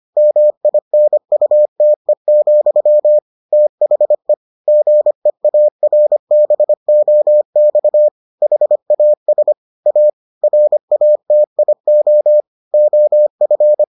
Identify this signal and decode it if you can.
Morse code, heard as a keyed tone, about 25 wpm, 600 Hz. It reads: MINUTE, THE GEARBOX HAS A RATIO OF